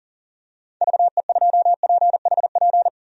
Morse code, heard as a keyed tone, about 40 wpm, 700 Hz. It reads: VE2PHP
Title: DXCC-100 Call Signs - 6 Characters - Repeated 2x - 40wpm